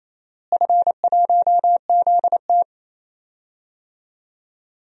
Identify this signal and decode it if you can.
Morse code, heard as a keyed tone, about 27 wpm, 700 Hz. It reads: F1ZT
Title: DXCC-100 Call Signs - 4 Characters - Repeated 2x - 28wpm